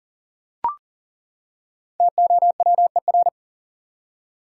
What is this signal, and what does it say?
Morse code, heard as a keyed tone, about 40 wpm, 700 Hz. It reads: TOWER
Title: Five Letter Words - 40wpm